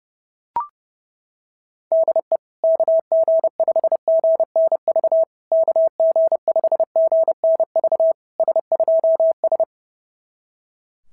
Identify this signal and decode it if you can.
Morse code, heard as a keyed tone, about 30 wpm, 660 Hz. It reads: DE KG5GNV KG5GNV S2S
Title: SOTA QSO Practice 30wpm